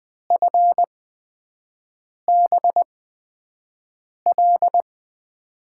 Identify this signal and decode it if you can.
Morse code, heard as a keyed tone, about 21 wpm, spaced as for 5 wpm, 700 Hz. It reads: FBL
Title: USA Call Signs - 6 Characters - 8x Character Spacing - 20wpm